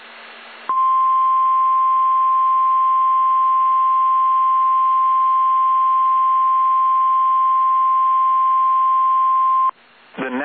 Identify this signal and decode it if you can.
Emergency Alert System (SAME) tones (the 1050 Hz attention tone). It is heard from 0.7 to 9.8 s.